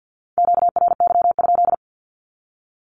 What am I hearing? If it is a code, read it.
Morse code, heard as a keyed tone, about 48 wpm, 700 Hz. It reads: QRY?
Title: Q-Codes 50wpm